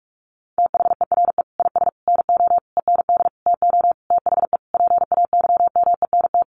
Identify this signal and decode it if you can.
Morse code, heard as a keyed tone, about 44 wpm, 700 Hz. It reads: THERE IS NO END TO THE PAYMENT
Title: Sentences from Top 1200 Words 45wpm